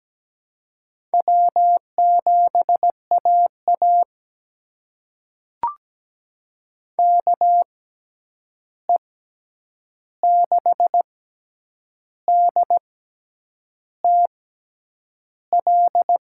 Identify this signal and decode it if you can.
Morse code, heard as a keyed tone, about 17 wpm, spaced as for 8 wpm, 700 Hz. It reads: W7AA KE6DTL
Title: USA Call Signs - 6x Character Spacing - 17wpm